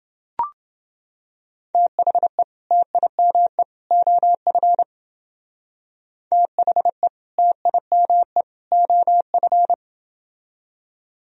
Morse code, heard as a keyed tone, about 28 wpm, 700 Hz. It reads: THE TIME OF THE TIME OF